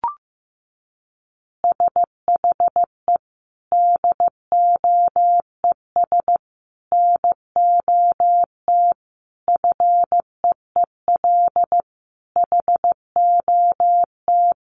Morse code, heard as a keyed tone, about 15 wpm, 700 Hz. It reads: SHE DOES NOT FEEL HOT
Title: Additional Easy Phrases 15wpm